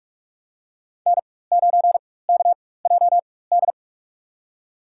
Morse code, heard as a keyed tone, about 46 wpm, spaced as for 21 wpm, 700 Hz. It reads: N9KJD
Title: USA Call Signs - 5 Characters - 4x Character Spacing - 45wpm